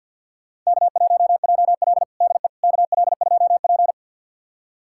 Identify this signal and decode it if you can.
Morse code, heard as a keyed tone, about 52 wpm, 700 Hz. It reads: K1JL DE KL2P
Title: USA Callsign DE Callsign - 4 Characters - 50wpm